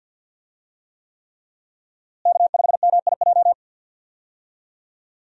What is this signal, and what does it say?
Morse code, heard as a keyed tone, about 50 wpm, 700 Hz. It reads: K5MIJ
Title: USA Call Signs - 5 Characters - Repeated 3x - 50wpm